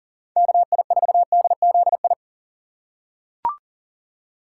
Morse code, heard as a keyed tone, about 40 wpm, 700 Hz. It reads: KI4DZI
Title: USA Call Signs - 6 Characters - 3x Character Spacing - 40wpm